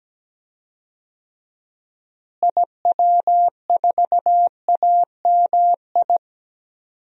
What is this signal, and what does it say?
Morse code, heard as a keyed tone, about 17 wpm, 700 Hz. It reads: IW4AMI
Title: DXCC-100 Call Signs - 6 Characters - Repeated 3x - 17wpm